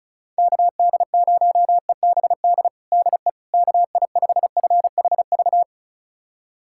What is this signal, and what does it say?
Morse code, heard as a keyed tone, about 34 wpm, 690 Hz. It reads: KD0EBD DE KI5FHV
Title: USA Callsign DE Callsign - 6 Characters - 35wpm